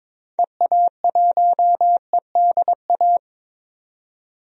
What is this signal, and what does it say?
Morse code, heard as a keyed tone, about 22 wpm, 700 Hz. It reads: EA1EDA
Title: DXCC-100 Call Signs - 6 Characters - 22wpm